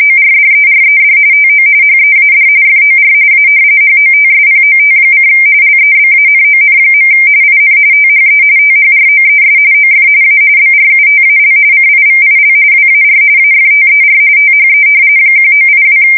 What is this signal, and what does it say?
RTTY, about 45.45 baud. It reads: ORDERED TO HEAD FOR A HOTSPOT: LEBANON. IN THE THROES OF A BRUTAL CIVIL WAR AFTER THE 1982
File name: fsk170 / battleshipnj